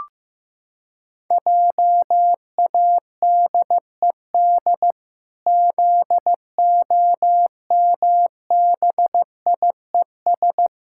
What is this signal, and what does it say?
Morse code, heard as a keyed tone, about 15 wpm, 700 Hz. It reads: JADED ZOMBIES
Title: Pangram Challenge v2 - 15wpm